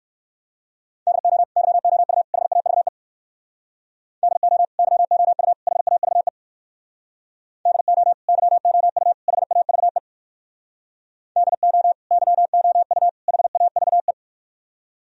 Morse code, heard as a keyed tone, about 57 wpm, 700 Hz. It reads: DO YOU HAVE DO YOU HAVE DO YOU HAVE DO YOU HAVE